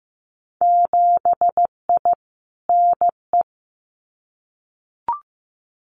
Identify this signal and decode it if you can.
Morse code, heard as a keyed tone, about 15 wpm, 700 Hz. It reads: 7I NE
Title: Winter Field Day Exchanges Encore 15wpm